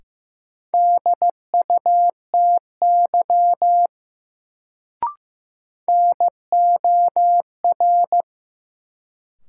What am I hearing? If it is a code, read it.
Morse code, heard as a keyed tone, about 15 wpm, 700 Hz. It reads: DUTY NOR
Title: Sentences from Top 1500 Words - New Word Drill - 15wpm